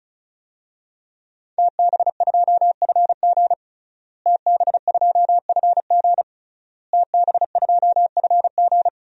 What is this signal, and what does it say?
Morse code, heard as a keyed tone, about 36 wpm, 690 Hz. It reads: TB2FG TB2FG TB2FG
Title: DXCC-100 Call Signs - 5 Characters - Repeated 3x - 35wpm